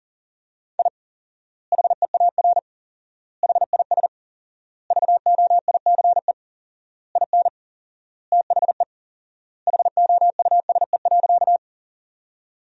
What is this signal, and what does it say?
Morse code, heard as a keyed tone, about 37 wpm, 700 Hz. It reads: I HEAR HIS VOICE IN THE HOUSE.
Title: 4x Word-Spacing - Sentences from Top 500 Words - 40wpm